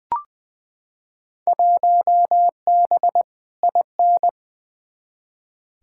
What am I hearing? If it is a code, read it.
Morse code, heard as a keyed tone, about 20 wpm, 700 Hz. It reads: 1B IN
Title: Field Day Exchanges - 20wpm